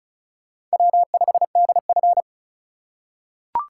Morse code, heard as a keyed tone, about 34 wpm, 700 Hz. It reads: W5DF
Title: USA Call Signs - 4x Character Spacing - 35wpm